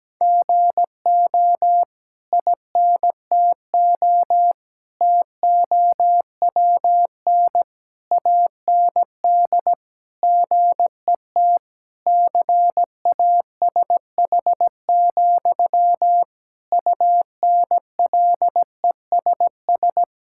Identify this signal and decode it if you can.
Morse code, heard as a keyed tone, about 17 wpm, 700 Hz. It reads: GO INTO TOWN AND GET CASH, UNLESS
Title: Sentences from Top 1000 Words - Review Part 6 - 17wpm